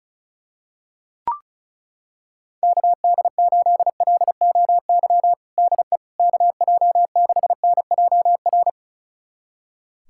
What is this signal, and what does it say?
Morse code, heard as a keyed tone, about 36 wpm, 700 Hz. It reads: KD8LOY DE KJ6NJR
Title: USA Callsign DE Callsign - 6 Characters - 35wpm